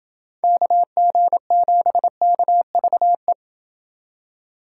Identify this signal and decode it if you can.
Morse code, heard as a keyed tone, about 27 wpm, 700 Hz. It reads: KG7KVE